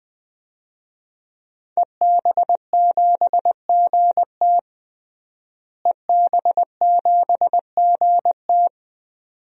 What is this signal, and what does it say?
Morse code, heard as a keyed tone, about 20 wpm, 700 Hz. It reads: EB7GT EB7GT
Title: DXCC-100 Call Signs - 5 Characters - Repeated 2x - 20wpm